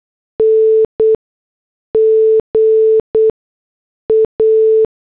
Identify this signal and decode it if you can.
Morse code, heard as a keyed tone, about 8 wpm, spaced as for 6 wpm, 440 Hz. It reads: NGA